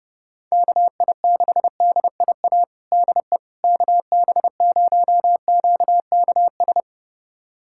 Morse code, heard as a keyed tone, about 30 wpm, 700 Hz. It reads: KI6DIA DE KB0QKS
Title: USA Callsign DE Callsign 30wpm